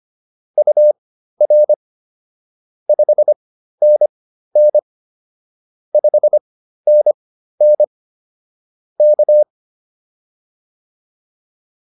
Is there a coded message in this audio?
Yes: Morse code from 0.6 to 9.4 s, heard as a keyed tone, about 25 wpm, spaced as for 13 wpm, 600 Hz. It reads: UR 5NN 5NN K